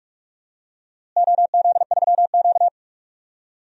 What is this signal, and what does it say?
Morse code, heard as a keyed tone, about 46 wpm, 700 Hz. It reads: OZ3Q
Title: DXCC-100 Call Signs - 4 Characters - Repeated 3x - 45wpm